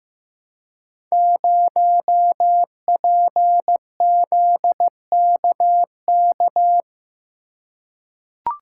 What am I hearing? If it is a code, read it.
Morse code, heard as a keyed tone, about 15 wpm, 700 Hz. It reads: 0PZKK
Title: Code Groups - 5 Letter:Numbers - 15wpm